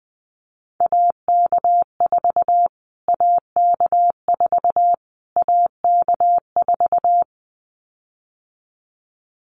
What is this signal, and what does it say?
Morse code, heard as a keyed tone, about 20 wpm, 700 Hz. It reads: AK4 AK4 AK4
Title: Callsign Prefixes for Top 25 DXCC - Repeated 3x - 20wpm